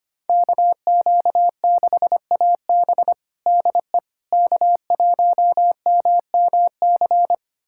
Morse code, heard as a keyed tone, about 25 wpm, 700 Hz. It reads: KQ6AB DE K1MMC
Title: USA Callsign DE Callsign - 5 Characters - 25wpm